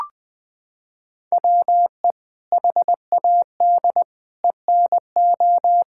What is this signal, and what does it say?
Morse code, heard as a keyed tone, about 21 wpm, 700 Hz. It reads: WE HAD ENO
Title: Sentences from Top 1600 Words 20wpm